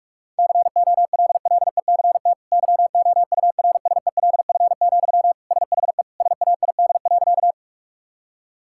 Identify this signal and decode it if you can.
Morse code, heard as a keyed tone, about 46 wpm, 700 Hz. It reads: COLLECT YOURSELF, SHE SAID.
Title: Sentences from Top 1600 Words 45wpm